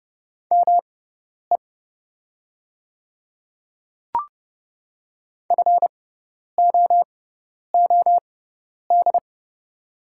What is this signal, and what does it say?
Morse code, heard as a keyed tone, about 30 wpm, spaced as for 10 wpm, 700 Hz. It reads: ME FOOD
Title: 6x Character Spacing - Top 500 Words - 30wpm